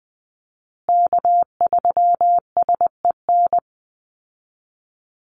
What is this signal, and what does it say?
Morse code, heard as a keyed tone, about 20 wpm, 700 Hz. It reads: K3SEN
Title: Call Signs Repeated 3x - 20wpm